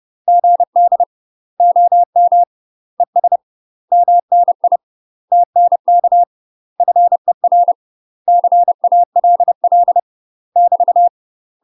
Morse code, heard as a keyed tone, about 30 wpm, 700 Hz. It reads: GD OM ES MNI TNK FER CALL <BT>
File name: UK MASTER QSOs TRAINING / 30WPM / UK MASTER QSOs 0026